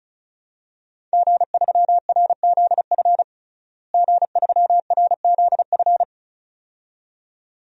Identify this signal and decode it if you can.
Morse code, heard as a keyed tone, about 34 wpm, 700 Hz. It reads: G3RZF G3RZF
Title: DXCC-100 Call Signs - 5 Characters - Repeated 2x - 35wpm